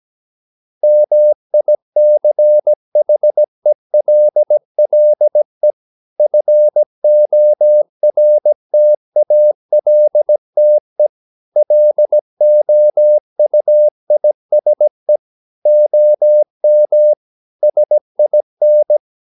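Morse code, heard as a keyed tone, about 17 wpm, 600 Hz. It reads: MICHELLE FORTALTE LOUISE OM SIN
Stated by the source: Slow (QRS) Morse practice at 17 wpm, from CW med Gnister.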